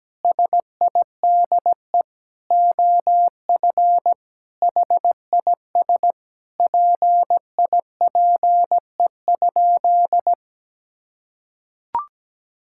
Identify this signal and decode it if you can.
Morse code, heard as a keyed tone, about 17 wpm, 700 Hz. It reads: SIDE OF HIS PIPE?
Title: Sentences from Top 1600 Words 17wpm